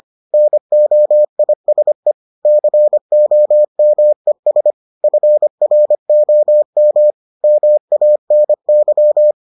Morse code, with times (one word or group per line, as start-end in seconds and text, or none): 0.3-2.1 NOISE
2.5-4.7 COMES
5.0-7.1 FROM
7.4-9.3 MANY